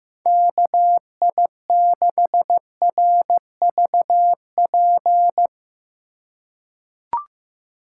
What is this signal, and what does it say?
Morse code, heard as a keyed tone, about 15 wpm, 700 Hz. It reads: KI6RVP
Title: USA Call Signs - 6 Characters - 3x Character Spacing - 15wpm